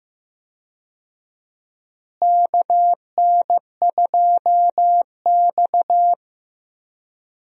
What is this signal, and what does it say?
Morse code, heard as a keyed tone, about 15 wpm, 700 Hz. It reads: KN2X
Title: USA Call Signs - 4 Characters - Repeated 3x - 15wpm